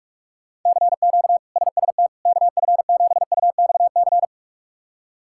Morse code, heard as a keyed tone, about 44 wpm, 700 Hz. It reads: CQ SST KF7UXC
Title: K1USN SST QSO Practice - 45wpm